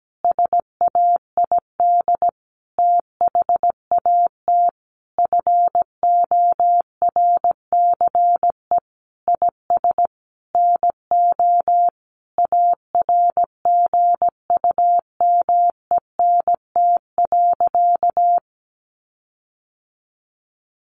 Morse code, heard as a keyed tone, about 17 wpm, 700 Hz. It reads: SAID THAT FORCE IS NO ARGUMENT.
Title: Sentences from Top 1400 Words 17wpm